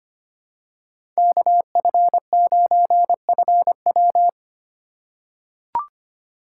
Morse code, heard as a keyed tone, about 25 wpm, 700 Hz. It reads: KF9FW
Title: USA Call Signs - 3x Character Spacing - 25wpm